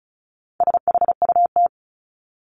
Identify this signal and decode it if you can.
Morse code, heard as a keyed tone, about 34 wpm, 700 Hz. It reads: SHUT
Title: Top 1751-2000 Words 35wpm